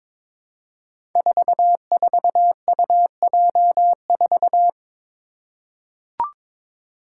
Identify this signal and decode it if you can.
Morse code, heard as a keyed tone, about 22 wpm, 700 Hz. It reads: 44UJ4
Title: Five Challenging Letter-Number Combinations 22wpm